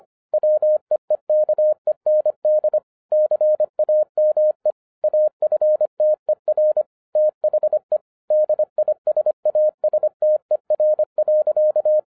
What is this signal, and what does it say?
Morse code, heard as a keyed tone, about 25 wpm, 610 Hz. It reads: WEEKEND CAME AFTER THE DISASTER.